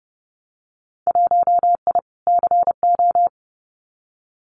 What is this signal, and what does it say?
Morse code, heard as a keyed tone, about 30 wpm, 700 Hz. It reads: 1I CO
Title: Winter Field Day Exchanges Encore 30wpm